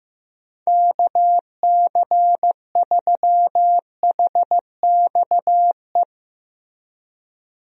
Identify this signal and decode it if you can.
Morse code, heard as a keyed tone, about 15 wpm, 700 Hz. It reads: KC3HXE
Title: USA Call Signs - 6 Characters - Repeated 3x - 15wpm